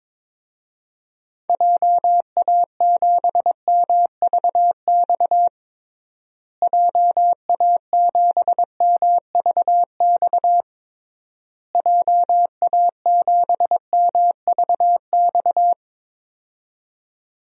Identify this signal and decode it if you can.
Morse code, heard as a keyed tone, about 22 wpm, 700 Hz. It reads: JA7MVX JA7MVX JA7MVX
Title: DXCC-100 Call Signs - 6 Characters - Repeated 3x - 22wpm